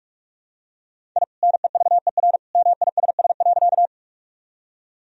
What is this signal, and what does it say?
Morse code, heard as a keyed tone, about 46 wpm, 700 Hz. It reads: I NEVER MISS.
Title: Sentences from Top 1000 Words - Review Part 3 - 45wpm